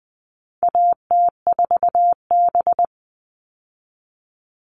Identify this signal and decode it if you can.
Morse code, heard as a keyed tone, about 20 wpm, 700 Hz. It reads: AT4B
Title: All Cut Numbers - 4-Digit 20wpm